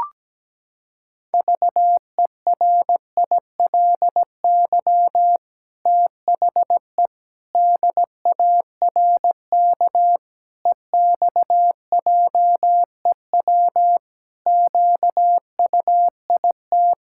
Morse code, heard as a keyed tone, about 17 wpm, 700 Hz. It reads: VERILY THE DARK EXJEW QUIT
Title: Pangram Challenge v2 - 17wpm